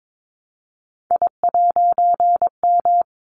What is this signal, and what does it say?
Morse code, heard as a keyed tone, about 22 wpm, 700 Hz. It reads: I'M
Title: A Sound Of Thunder 22wpm